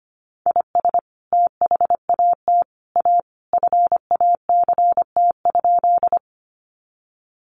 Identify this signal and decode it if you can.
Morse code, heard as a keyed tone, about 25 wpm, 700 Hz. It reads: IS THAT A FACT?
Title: Sentences from Top 500 Words 25wpm